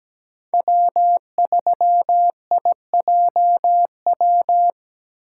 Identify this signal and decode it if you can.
Morse code, heard as a keyed tone, about 17 wpm, 700 Hz. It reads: W3IJW
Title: USA Call Signs - 5 Characters - 3x Character Spacing - 17wpm